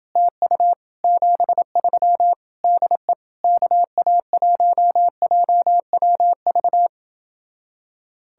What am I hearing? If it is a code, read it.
Morse code, heard as a keyed tone, about 28 wpm, 700 Hz. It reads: TU 73 DE KA1JWV